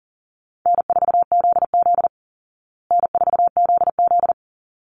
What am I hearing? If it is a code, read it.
Morse code, heard as a keyed tone, about 40 wpm, 700 Hz. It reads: N4ZZ N4ZZ
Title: Call Signs Repeated 3x - 40wpm